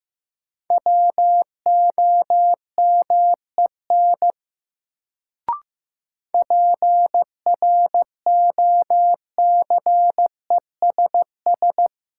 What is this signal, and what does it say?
Morse code, heard as a keyed tone, about 15 wpm, 700 Hz. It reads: WOMEN PROCESS
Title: Sets of 3 Words - New Word Drill - 15wpm